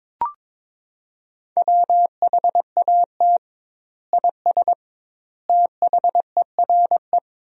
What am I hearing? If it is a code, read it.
Morse code, heard as a keyed tone, about 22 wpm, 700 Hz. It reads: WHAT IS THERE
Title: 2x Word-Spacing - Sentences from Top 100 Words - 22wpm